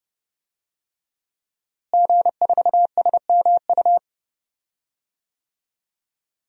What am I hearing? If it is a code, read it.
Morse code, heard as a keyed tone, about 30 wpm, 700 Hz. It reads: G4SMU
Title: DXCC-100 Call Signs - 5 Characters - Repeated 3x - 30wpm